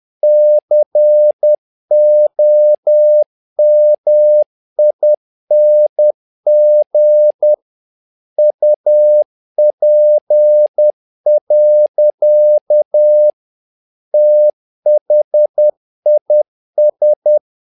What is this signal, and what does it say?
Morse code, heard as a keyed tone, about 10 wpm, 600 Hz. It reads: COMING UP. THIS
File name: learning center / tower trouble / tower trouble 10wpm